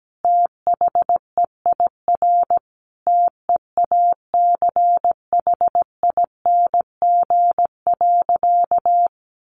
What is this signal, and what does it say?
Morse code, heard as a keyed tone, about 17 wpm, 700 Hz. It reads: THEIR TEACHING.
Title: Sentences from Top 1100 Words 17wpm